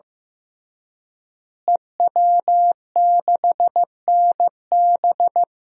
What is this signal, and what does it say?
Morse code, heard as a keyed tone, about 15 wpm, 700 Hz. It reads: EW6NB
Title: DXCC-100 Call Signs - 5 Characters - Repeated 3x - 15wpm